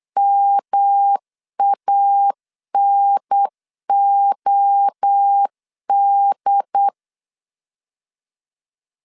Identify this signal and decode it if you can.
Morse code, heard as a keyed tone, about 8 wpm, 800 Hz. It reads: MANOD
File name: CW / DET / 1 3 1500C S B 8wpm 4141